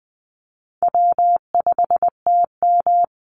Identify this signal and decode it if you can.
Morse code, heard as a keyed tone, about 20 wpm, 700 Hz. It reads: W5TM
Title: Easy Call Signs Repeated 3x - 20wpm